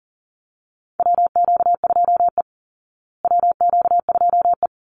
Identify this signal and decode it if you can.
Morse code, heard as a keyed tone, about 40 wpm, 700 Hz. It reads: WQ2E WQ2E
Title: Call Signs Repeated 3x - 40wpm